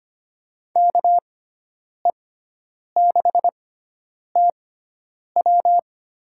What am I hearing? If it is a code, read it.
Morse code, heard as a keyed tone, about 25 wpm, spaced as for 9 wpm, 700 Hz. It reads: KE6TW
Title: USA Call Signs - 6x Character Spacing - 25wpm